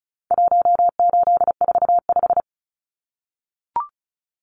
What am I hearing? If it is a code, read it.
Morse code, heard as a keyed tone, about 34 wpm, 700 Hz. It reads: 1845
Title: Numbers - 4-Digit 35wpm